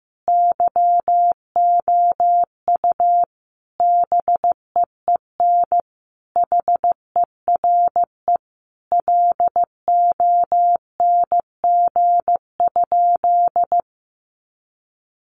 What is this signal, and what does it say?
Morse code, heard as a keyed tone, about 15 wpm, 700 Hz. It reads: YOU BEEN HERE LONG?
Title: Sentences from Top 200 Words 15wpm